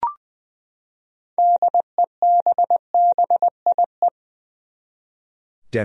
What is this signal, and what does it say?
Morse code, heard as a keyed tone, about 20 wpm, 700 Hz. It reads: DEBBIE
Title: USA Common Names - 20wpm